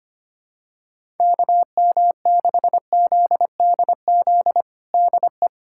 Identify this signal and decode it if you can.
Morse code, heard as a keyed tone, about 25 wpm, 700 Hz. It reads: KM6ZDZ DE
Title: USA Callsign DE Callsign 25wpm